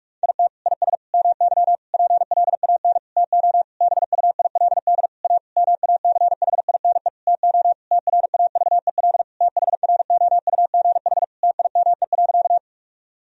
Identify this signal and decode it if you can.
Morse code, heard as a keyed tone, about 46 wpm, 700 Hz. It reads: IT IS MY PLAN TO BUILD A MACHINE TO TRAVEL THROUGH TIME.
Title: Sentences from Top 1000 Words - Review Part 2 - 45wpm